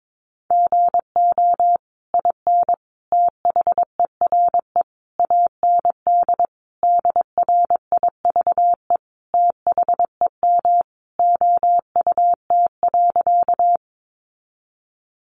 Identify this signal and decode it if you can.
Morse code, heard as a keyed tone, about 22 wpm, 700 Hz. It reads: GO IN THERE AND DRIVE THEM OUT.
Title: Sentences from Top 500 Words 22wpm